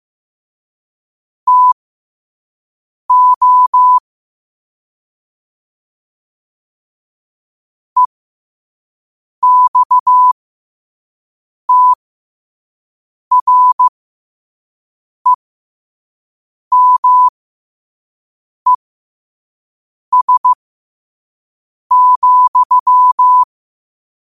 Morse code, heard as a keyed tone, about 15 wpm, spaced as for 5 wpm, 1000 Hz. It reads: TO EXTREMES,
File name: qotdfiles / 5wpm